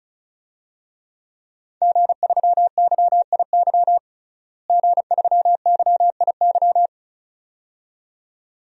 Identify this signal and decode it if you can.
Morse code, heard as a keyed tone, about 34 wpm, 700 Hz. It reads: G3YIY G3YIY
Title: DXCC-100 Call Signs - 5 Characters - Repeated 2x - 35wpm